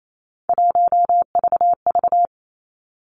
Morse code, heard as a keyed tone, about 28 wpm, 700 Hz. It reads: 1VV
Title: Three Challenging Letter-Number Combinations 28wpm